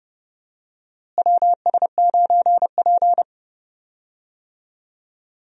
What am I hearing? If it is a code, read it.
Morse code, heard as a keyed tone, about 30 wpm, 700 Hz. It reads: WS9P
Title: USA Call Signs - 4 Characters - Repeated 2x - 30wpm